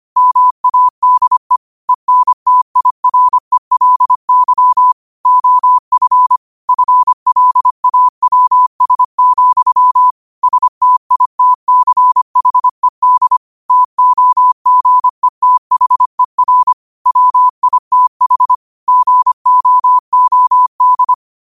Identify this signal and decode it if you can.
Morse code, heard as a keyed tone, about 25 wpm, 1000 Hz. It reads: MADE ENTIRELY OF FLAWS, STITCHED TOGETHER WITH GOOD
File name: qotdfiles / 25wpm